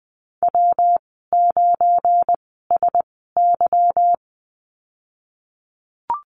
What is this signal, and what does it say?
Morse code, heard as a keyed tone, about 20 wpm, spaced as for 15 wpm, 700 Hz. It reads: W9SY
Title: 193 Lesson 31g - Letter v - Call Signs 20wpm